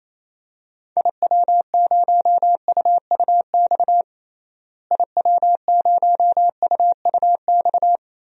Morse code, heard as a keyed tone, about 28 wpm, 700 Hz. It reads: IW0UUX IW0UUX
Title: DXCC-100 Call Signs - 6 Characters - Repeated 2x - 28wpm